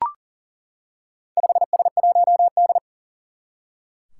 Morse code, heard as a keyed tone, about 40 wpm, 700 Hz. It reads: 5S1D
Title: Four Challenging Letter-Number Combinations 40wpm